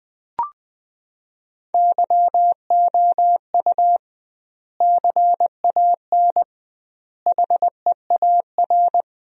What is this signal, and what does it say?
Morse code, heard as a keyed tone, about 20 wpm, 700 Hz. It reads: YOU CAN HEAR
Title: 2x Word-Spacing - Sentences from Top 500 Words - 20wpm